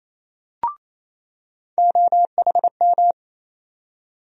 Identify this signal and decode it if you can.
Morse code, heard as a keyed tone, about 28 wpm, 700 Hz. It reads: OHM
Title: USA Call Sign Postfixes - 3 Letters 28wpm